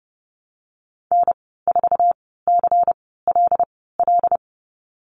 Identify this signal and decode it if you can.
Morse code, heard as a keyed tone, about 30 wpm, spaced as for 17 wpm, 700 Hz. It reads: N4CLL
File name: course / 30:10wpm / 116 Lesson 19g - Letter w - Call Signs 30wpm